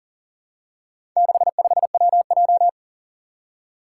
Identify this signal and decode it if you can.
Morse code, heard as a keyed tone, about 42 wpm, 700 Hz. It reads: 65WJ
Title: Four Challenging Letter-Number Combinations 40wpm